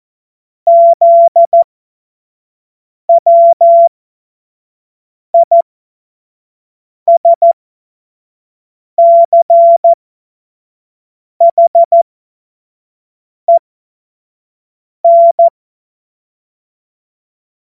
Morse code, heard as a keyed tone, about 14 wpm, spaced as for 5 wpm, 680 Hz. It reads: ZWISCHEN
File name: Morsekurs / Witze / Witz 16. Februar 2021, 25 F70